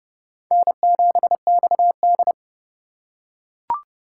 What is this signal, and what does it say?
Morse code, heard as a keyed tone, about 31 wpm, 700 Hz. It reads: N7XD
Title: USA Call Signs - 4 Characters - 3x Character Spacing - 30wpm